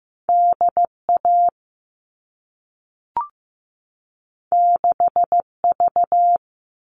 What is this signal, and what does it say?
Morse code, heard as a keyed tone, about 15 wpm, 700 Hz. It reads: DA 6V
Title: All Cut Numbers - 2-Digit 15wpm